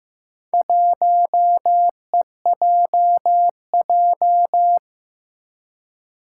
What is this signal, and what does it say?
Morse code, heard as a keyed tone, about 15 wpm, 700 Hz. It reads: 1EJJ
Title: Four Challenging Letter-Number Combinations 15wpm